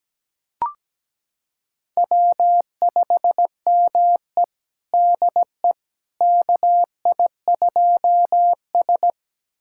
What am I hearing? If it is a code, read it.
Morse code, heard as a keyed tone, about 17 wpm, 700 Hz. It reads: W5ME DE KI2S
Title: USA Callsign DE Callsign - 4 Characters - 17wpm